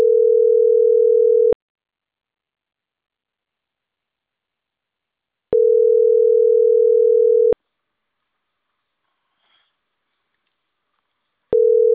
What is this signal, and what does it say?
Ringback tone (North America), 440 + 480 Hz.